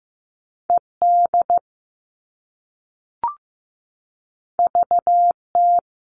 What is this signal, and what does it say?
Morse code, heard as a keyed tone, about 15 wpm, 700 Hz. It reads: ED VT
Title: All Cut Numbers - 2-Digit 15wpm